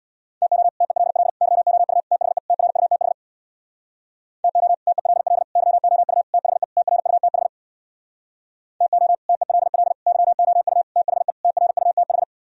Morse code, heard as a keyed tone, about 64 wpm, 700 Hz. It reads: TO TELL YOU THE TRUTH TO TELL YOU THE TRUTH TO TELL YOU THE TRUTH